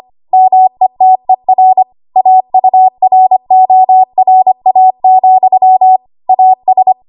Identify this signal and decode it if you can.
Morse code, heard as a keyed tone, about 25 wpm, 750 Hz. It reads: METER AURORA, AS
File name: Morse / 25WPM